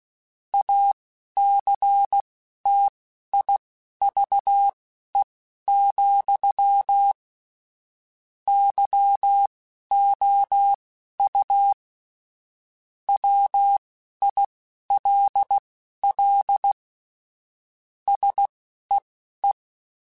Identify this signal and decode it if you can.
Morse code, heard as a keyed tone, about 16 wpm, spaced as for 12 wpm, 800 Hz. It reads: ACTIVE, YOU WILL SEE
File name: Technical Subjects / Technical Subjects-mp3s / OR Gate @10